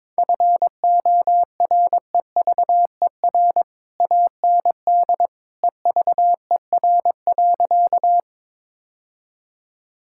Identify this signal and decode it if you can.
Morse code, heard as a keyed tone, about 21 wpm, 700 Hz. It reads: FOREVER AND EVER.
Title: Sentences from Top 1600 Words 22wpm